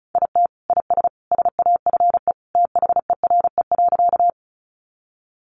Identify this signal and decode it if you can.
Morse code, heard as a keyed tone, about 34 wpm, 700 Hz. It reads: IT IS SAFE THERE.
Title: Easy Phrases 35wpm